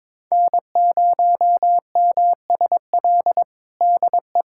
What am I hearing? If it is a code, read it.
Morse code, heard as a keyed tone, about 21 wpm, 700 Hz. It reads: N0MSL DE
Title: USA Callsign DE Callsign - 5 Characters - 22wpm